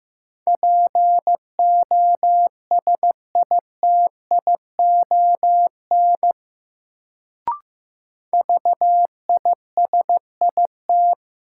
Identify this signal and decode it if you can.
Morse code, heard as a keyed tone, about 15 wpm, 700 Hz. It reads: POSITION VISIT
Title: Sentences from Top 900 Words - New Word Drill - 15wpm